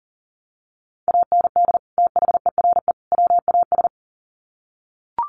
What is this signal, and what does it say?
Morse code, heard as a keyed tone, about 37 wpm, 700 Hz. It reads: AND THERE WAS
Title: Sets of 3 Words 40wpm